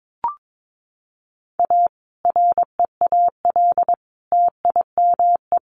Morse code, heard as a keyed tone, about 22 wpm, 700 Hz. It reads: A REAL TIME
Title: A Sound Of Thunder 22wpm